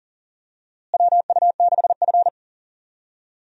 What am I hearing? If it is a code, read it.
Morse code, heard as a keyed tone, about 40 wpm, 700 Hz. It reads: WU6F
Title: USA Call Signs - 4 Characters - Repeated 3x - 40wpm